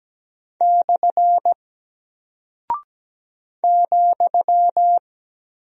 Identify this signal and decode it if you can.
Morse code, heard as a keyed tone, about 17 wpm, 700 Hz. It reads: / ,
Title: Single Character (Letter:Number:Punctuation:Prosign) Alt 17wpm